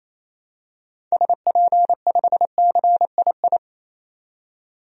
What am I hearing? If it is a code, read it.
Morse code, heard as a keyed tone, about 28 wpm, 700 Hz. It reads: SP5CII
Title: DXCC-100 Call Signs - 6 Characters - 28wpm